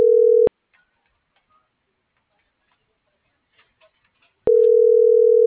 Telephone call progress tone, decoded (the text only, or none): ringback tone